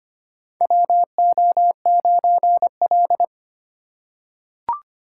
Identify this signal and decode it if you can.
Morse code, heard as a keyed tone, about 25 wpm, 700 Hz. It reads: WO9L
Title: USA Call Signs - 4 Characters - 3x Character Spacing - 25wpm